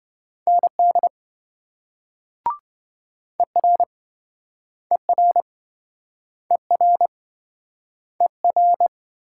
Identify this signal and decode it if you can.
Morse code, heard as a keyed tone, about 26 wpm, 700 Hz. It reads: ND ER ER ER ER